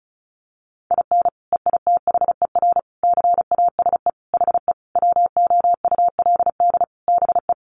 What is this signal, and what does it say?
Morse code, heard as a keyed tone, about 34 wpm, 700 Hz. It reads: IN EITHER CASE HE WOULD BE
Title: Sentences from Top 1300 Words 35wpm